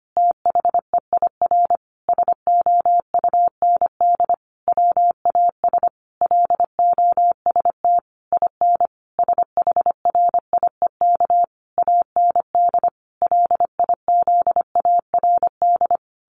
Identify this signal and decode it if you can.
Morse code, heard as a keyed tone, about 25 wpm, 700 Hz. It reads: THEIR SOUND WAS LOST IN SHRIEK AND LIZARD
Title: A Sound Of Thunder 25wpm